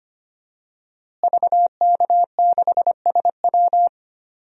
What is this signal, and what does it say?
Morse code, heard as a keyed tone, about 25 wpm, 700 Hz. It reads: VK6SW
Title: DXCC-100 Call Signs - 5 Characters - Repeated 3x - 25wpm